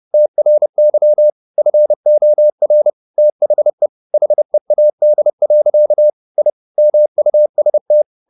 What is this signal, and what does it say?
Morse code, heard as a keyed tone, about 30 wpm, 600 Hz. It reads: TRY FOR THE HEAD. I MUST